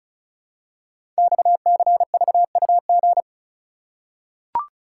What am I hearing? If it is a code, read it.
Morse code, heard as a keyed tone, about 36 wpm, 700 Hz. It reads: XCVUG
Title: Code Groups - 5 Letter:Numbers - 35wpm